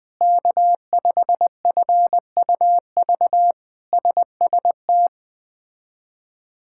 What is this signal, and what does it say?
Morse code, heard as a keyed tone, about 20 wpm, 700 Hz. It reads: K5FUV SST
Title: K1USN SST QSO Practice - 20wpm